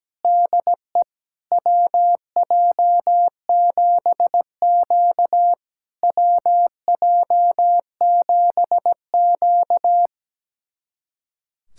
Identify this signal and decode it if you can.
Morse code, heard as a keyed tone, about 17 wpm, 700 Hz. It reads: DE WJ7Q WJ7Q
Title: SOTA QSO Practice 17wpm